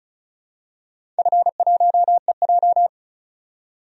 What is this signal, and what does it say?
Morse code, heard as a keyed tone, about 36 wpm, 700 Hz. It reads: F1EJ
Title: DXCC-100 Call Signs - 4 Characters - 35wpm